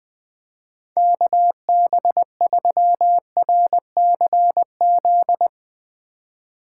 Morse code, heard as a keyed tone, about 20 wpm, 700 Hz. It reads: KB3RCZ
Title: Speed Racing - USA 2x3 Call Signs 17wpm